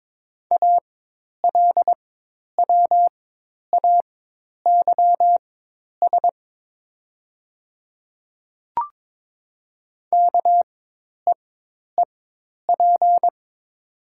Morse code, heard as a keyed tone, about 22 wpm, spaced as for 10 wpm, 700 Hz. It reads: ALWAYS KEEP
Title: 4x Character Spacing - Top 300 Words - 22wpm